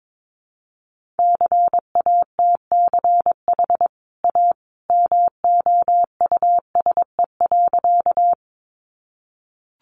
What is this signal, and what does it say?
Morse code, heard as a keyed tone, about 22 wpm, 700 Hz. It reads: CATCH A MOUSE.
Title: Sentences from Top 1500 Words 22wpm